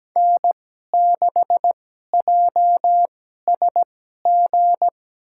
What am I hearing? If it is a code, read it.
Morse code, heard as a keyed tone, about 17 wpm, spaced as for 12 wpm, 700 Hz. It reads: N6JSG
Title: USA Call Signs - 5 Characters - 2x Character Spacing - 17wpm